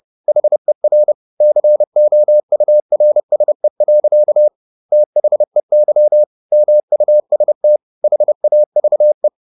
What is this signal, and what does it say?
Morse code, heard as a keyed tone, about 30 wpm, 600 Hz. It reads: HER COURSE. THEY MUST HAVE